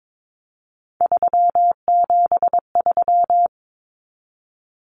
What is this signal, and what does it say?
Morse code, heard as a keyed tone, about 22 wpm, 700 Hz. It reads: 373
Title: Numbers - 3-Digit 22wpm